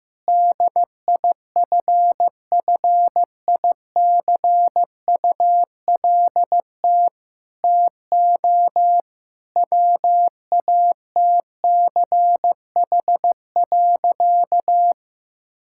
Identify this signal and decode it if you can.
Morse code, heard as a keyed tone, about 15 wpm, 700 Hz. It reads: DIFFICULT TO WATCH.
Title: Sentences from Top 1000 Words - Review Part 5 - 15wpm